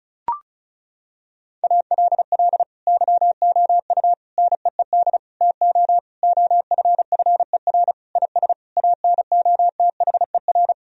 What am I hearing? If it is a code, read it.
Morse code, heard as a keyed tone, about 36 wpm, 700 Hz. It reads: ALL YOU NEED TO OFFER IS ANOTHER
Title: Sentences from Top 1000 Words - Review Part 4 - 35wpm